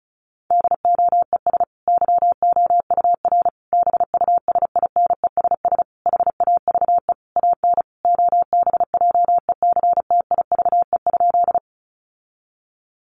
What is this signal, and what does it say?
Morse code, heard as a keyed tone, about 34 wpm, 700 Hz. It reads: DOES YOUR BUSINESS HAVE AN OBJECTIVE?
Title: Sentences from Top 1200 Words 35wpm